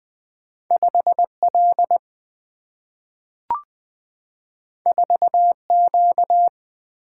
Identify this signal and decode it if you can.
Morse code, heard as a keyed tone, about 20 wpm, 700 Hz. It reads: 5L 4Q
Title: Single-Letter Single-Number 20wpm